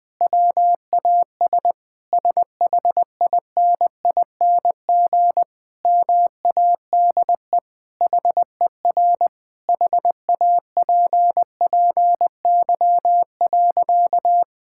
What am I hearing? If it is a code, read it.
Morse code, heard as a keyed tone, about 20 wpm, 700 Hz. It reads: WAS SHINING MADE HER HAPPY.
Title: Sentences with 4-Word Phrases - 20wpm